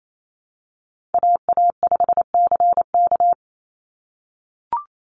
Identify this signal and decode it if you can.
Morse code, heard as a keyed tone, about 28 wpm, 700 Hz. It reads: AA5CK
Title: NA SOTA Chaser Callsigns 28wpm